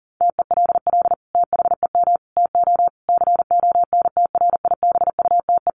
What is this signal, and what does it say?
Morse code, heard as a keyed tone, about 40 wpm, 700 Hz. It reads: TELL THEM TO CONTRIBUTE
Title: Sentences from Top 1400 Words 40wpm